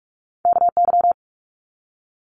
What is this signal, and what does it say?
Morse code, heard as a keyed tone, about 46 wpm, 700 Hz. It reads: KY
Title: Multi-letter Phonemes 45wpm